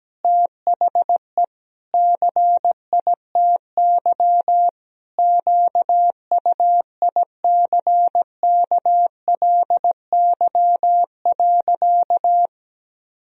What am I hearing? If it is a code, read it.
Morse code, heard as a keyed tone, about 17 wpm, 700 Hz. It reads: THE CITY QUICKLY.
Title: Sentences from Top 1000 Words - Review Part 1 - 17wpm